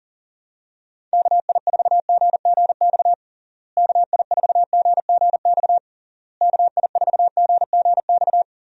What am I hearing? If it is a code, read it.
Morse code, heard as a keyed tone, about 37 wpm, 700 Hz. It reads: KI4GGX KI4GGX KI4GGX
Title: USA Call Signs - 6 Characters - Repeated 3x - 40wpm